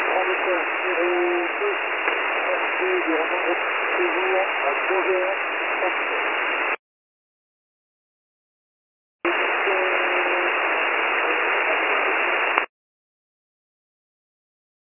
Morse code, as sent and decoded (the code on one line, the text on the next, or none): none